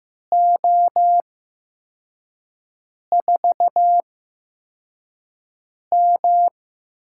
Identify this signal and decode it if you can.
Morse code, heard as a keyed tone, about 15 wpm, spaced as for 4 wpm, 700 Hz. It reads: O4M
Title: USA Call Signs - 8x Character Spacing - 15wpm